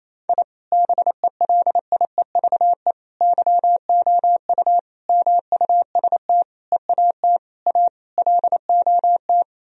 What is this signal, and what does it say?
Morse code, heard as a keyed tone, about 28 wpm, 700 Hz. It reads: I BELIEVE YOU MUST EAT A LOT
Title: Sentences from Top 1000 Words - Review Part 3 - 28wpm